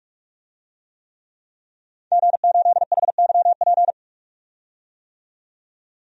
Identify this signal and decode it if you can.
Morse code, heard as a keyed tone, about 46 wpm, 700 Hz. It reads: G8HYP
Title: DXCC-100 Call Signs - 5 Characters - 45wpm